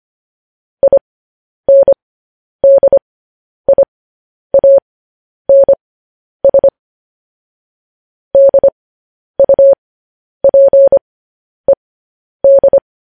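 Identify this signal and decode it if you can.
Morse code, heard as a keyed tone, about 25 wpm, spaced as for 10 wpm, 550 Hz. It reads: INDIANS DUPED
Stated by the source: Morse code news headlines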